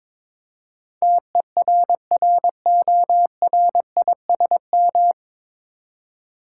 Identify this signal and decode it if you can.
Morse code, heard as a keyed tone, about 22 wpm, 700 Hz. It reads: TERRORISM
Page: Morse Code Ninja practice 22wpm